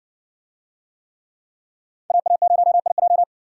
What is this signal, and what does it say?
Morse code, heard as a keyed tone, about 60 wpm, 700 Hz. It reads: AA0IJ